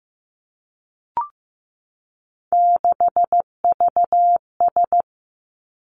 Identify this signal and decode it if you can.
Morse code, heard as a keyed tone, about 15 wpm, 700 Hz. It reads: 6VS
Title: Three Challenging Letter-Number Combinations 15wpm